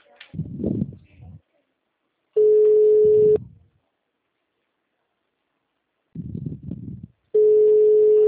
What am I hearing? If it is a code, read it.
Ringback tone (Europe, 425 Hz).
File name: out- -2016- - - .9705